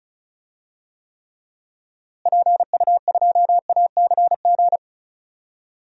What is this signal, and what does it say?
Morse code, heard as a keyed tone, about 36 wpm, 700 Hz. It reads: PU2ACG
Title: DXCC-100 Call Signs - 6 Characters - Repeated 2x - 35wpm